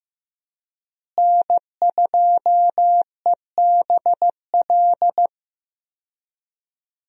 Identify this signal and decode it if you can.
Morse code, heard as a keyed tone, about 15 wpm, 700 Hz. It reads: N2EBL
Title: USA Call Signs - 5 Characters - Repeated 3x - 15wpm